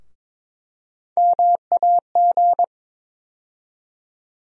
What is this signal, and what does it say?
Morse code, heard as a keyed tone, about 21 wpm, 700 Hz. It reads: MAG
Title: USA Call Sign Postfixes - 3 Letters 22wpm